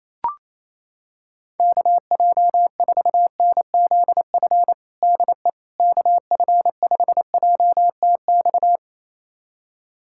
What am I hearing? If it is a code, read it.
Morse code, heard as a keyed tone, about 28 wpm, 700 Hz. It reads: KJ4NZF DE KF5JTX
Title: USA Callsign DE Callsign - 6 Characters - 28wpm